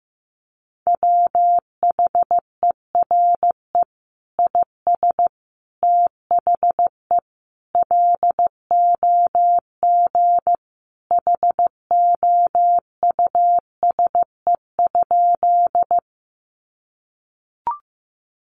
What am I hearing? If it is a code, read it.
Morse code, heard as a keyed tone, about 15 wpm, 700 Hz. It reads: WHERE IS THE LOG HOUSE?
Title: Sentences from Top 1400 Words 15wpm